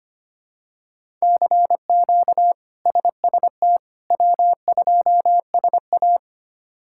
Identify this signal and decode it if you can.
Morse code, heard as a keyed tone, about 25 wpm, 700 Hz. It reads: CQ SST W2SA
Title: K1USN SST QSO Practice - 25wpm